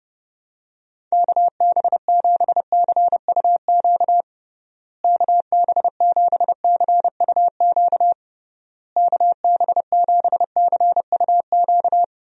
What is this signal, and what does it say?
Morse code, heard as a keyed tone, about 30 wpm, 700 Hz. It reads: KB7CUQ KB7CUQ KB7CUQ
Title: USA Call Signs - 6 Characters - Repeated 3x - 30wpm